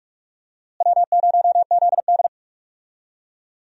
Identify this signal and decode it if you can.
Morse code, heard as a keyed tone, about 47 wpm, 700 Hz. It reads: W0ZD
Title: USA Call Signs - 4 Characters - Repeated 2x - 45wpm